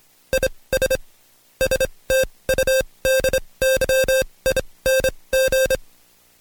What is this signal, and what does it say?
Morse code, heard as a keyed tone, about 26 wpm, spaced as for 20 wpm, 530 Hz. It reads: IS STUDYING
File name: Code Quick / AudioDownld2 / Seg136-Stepper9 19WPM